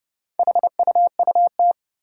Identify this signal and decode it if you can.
Morse code, heard as a keyed tone, about 30 wpm, 700 Hz. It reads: HUUT
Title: Four Challenging Letter-Number Combinations 30wpm